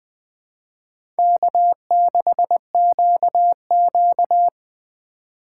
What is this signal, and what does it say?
Morse code, heard as a keyed tone, about 20 wpm, 700 Hz. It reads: K6QQ